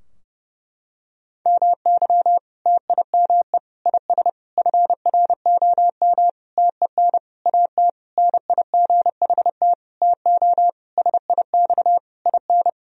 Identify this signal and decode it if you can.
Morse code, heard as a keyed tone, about 30 wpm, 700 Hz. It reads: MY TIME IS FROM TEN AT NIGHT TO SIX IN
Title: Sentences from Top 1000 Words - Review Part 2 - 30wpm